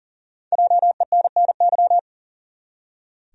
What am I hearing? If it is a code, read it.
Morse code, heard as a keyed tone, about 39 wpm, 700 Hz. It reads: JENNY
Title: USA Common Names - Max 5 Letters - 40wpm